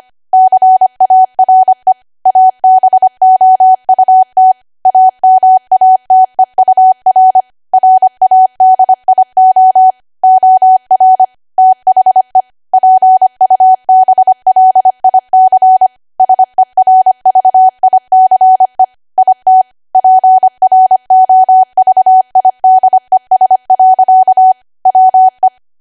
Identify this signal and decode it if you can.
Morse code, heard as a keyed tone, about 25 wpm, 750 Hz. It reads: CARE ABOUT AMATEUR RADIO OR THE PUBLIC SERVICE IT PROVIDES. WE
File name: W1AW-Level3/140218 25WPM